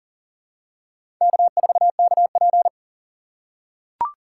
Morse code, heard as a keyed tone, about 40 wpm, 700 Hz. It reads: K4KP
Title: USA Call Signs - 4x Character Spacing - 40wpm